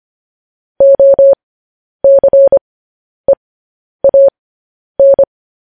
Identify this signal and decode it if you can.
Morse code, heard as a keyed tone, about 25 wpm, spaced as for 10 wpm, 550 Hz. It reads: OCEAN
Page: News Headlines in Morse Code at 10 WPM